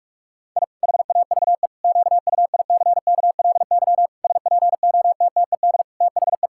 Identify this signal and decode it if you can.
Morse code, heard as a keyed tone, about 46 wpm, 700 Hz. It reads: I HAVE QUICKLY SPOTTED THE
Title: Pangram Challenge v2 - 45wpm